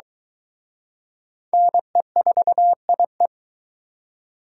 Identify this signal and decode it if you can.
Morse code, heard as a keyed tone, about 23 wpm, 700 Hz. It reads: NE4IE